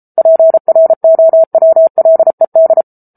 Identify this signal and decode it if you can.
Morse code, heard as a keyed tone, about 34 wpm, 650 Hz. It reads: PROWLED